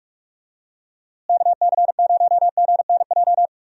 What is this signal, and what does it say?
Morse code, heard as a keyed tone, about 44 wpm, 700 Hz. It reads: KC0GNJ
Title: USA Call Signs - 6 Characters - Repeated 3x - 45wpm